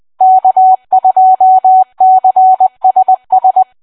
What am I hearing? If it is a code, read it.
Morse code, heard as a keyed tone, about 20 wpm, 750 Hz. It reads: K2CSS